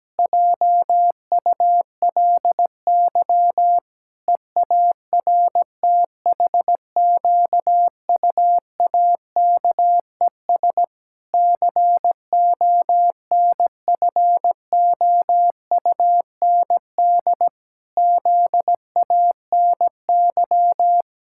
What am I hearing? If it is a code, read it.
Morse code, heard as a keyed tone, about 17 wpm, 700 Hz. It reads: JULY EARTHQUAKES CONFOUND ZANY
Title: Pangram Challenge v2 - 17wpm